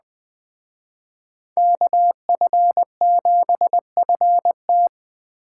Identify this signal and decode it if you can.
Morse code, heard as a keyed tone, about 19 wpm, 700 Hz. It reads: KF7FT